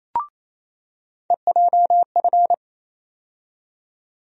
Morse code, heard as a keyed tone, about 28 wpm, 700 Hz. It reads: EJF
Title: USA Call Sign Postfixes - 3 Letters 28wpm